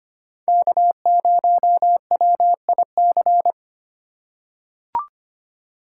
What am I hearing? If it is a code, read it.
Morse code, heard as a keyed tone, about 25 wpm, 700 Hz. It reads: K0WIC
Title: USA Call Signs - 5 Characters - 4x Character Spacing - 25wpm